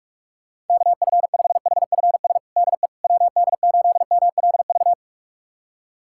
Morse code, heard as a keyed tone, about 46 wpm, 700 Hz. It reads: KF5HFS DE WD8MLV
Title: USA Callsign DE Callsign - 6 Characters - 45wpm